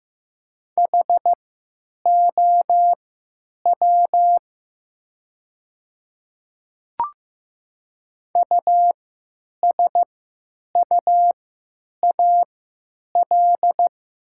Morse code, heard as a keyed tone, about 15 wpm, spaced as for 9 wpm, 700 Hz. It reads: HOW USUAL
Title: 3x Character Spacing - Top 500 Words - 15wpm